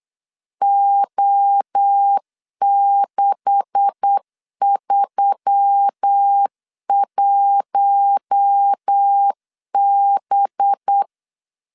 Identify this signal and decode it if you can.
Morse code, heard as a keyed tone, about 8 wpm, 800 Hz. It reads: O631B
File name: CW / DET / 1 9 1500C S C 8wpm 5653